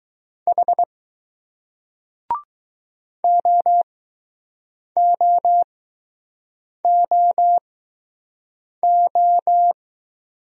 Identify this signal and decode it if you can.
Morse code, heard as a keyed tone, about 20 wpm, spaced as for 6 wpm, 700 Hz. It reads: H OOOO